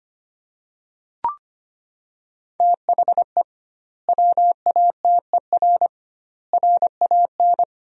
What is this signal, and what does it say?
Morse code, heard as a keyed tone, about 25 wpm, 700 Hz. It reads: THE WATER RAN
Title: 2x Word-Spacing - Sentences from Top 500 Words - 25wpm